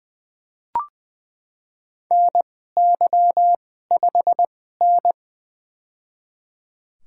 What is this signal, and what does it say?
Morse code, heard as a keyed tone, about 20 wpm, spaced as for 14 wpm, 700 Hz. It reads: NY5N
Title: USA Call Signs - 4 Characters - 2x Character Spacing - 20wpm